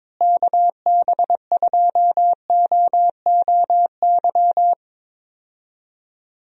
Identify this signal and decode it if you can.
Morse code, heard as a keyed tone, about 22 wpm, 700 Hz. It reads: KB2OOY
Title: USA Call Signs - 6 Characters - Repeated 2x - 22wpm